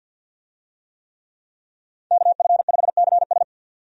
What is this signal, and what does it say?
Morse code, heard as a keyed tone, about 52 wpm, 700 Hz. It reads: KF5CS
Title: USA Call Signs - 5 Characters - 50wpm